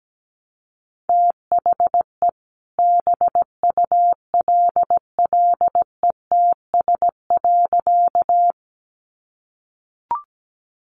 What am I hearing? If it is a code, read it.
Morse code, heard as a keyed tone, about 17 wpm, 700 Hz. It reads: THE BULLETS.
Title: A Sound Of Thunder 17wpm